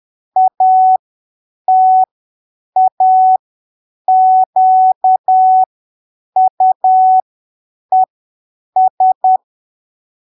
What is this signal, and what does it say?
Morse code, heard as a keyed tone, about 10 wpm, spaced as for 7 wpm, 750 Hz. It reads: ATAQUES
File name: arquivos / texto / texto 28 10ppm